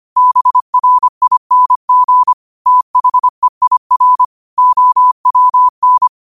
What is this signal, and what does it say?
Morse code, heard as a keyed tone, about 25 wpm, 1000 Hz. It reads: BRING THEIR OWN